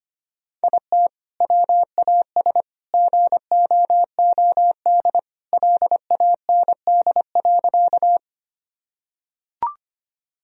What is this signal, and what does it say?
Morse code, heard as a keyed tone, about 25 wpm, 700 Hz. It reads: IT WAS GOOD LAND.
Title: Sentences from Top 1000 Words - Review Part 1 - 25wpm